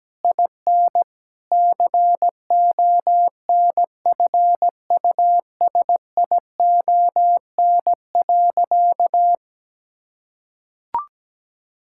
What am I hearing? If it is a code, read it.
Morse code, heard as a keyed tone, about 17 wpm, 700 Hz. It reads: IN CONFUSION.
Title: Sentences from Top 1600 Words 17wpm